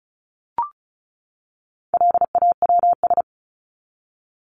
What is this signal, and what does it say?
Morse code, heard as a keyed tone, about 34 wpm, 700 Hz. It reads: LAWS
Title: ES:S Suffix - Top 100 Words with Base Words 35wpm